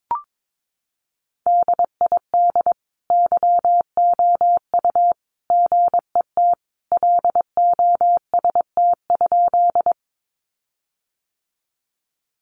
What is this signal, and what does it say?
Morse code, heard as a keyed tone, about 22 wpm, 700 Hz. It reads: DID YOU GET LOST?
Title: Sentences from Top 1400 Words 22wpm